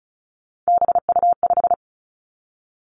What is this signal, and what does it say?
Morse code, heard as a keyed tone, about 36 wpm, 700 Hz. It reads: BU5
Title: Three Challenging Letter-Number Combinations 35wpm